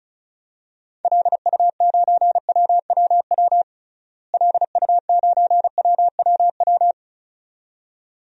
Morse code, heard as a keyed tone, about 36 wpm, 700 Hz. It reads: LU9WWW LU9WWW
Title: DXCC-100 Call Signs - 6 Characters - Repeated 2x - 35wpm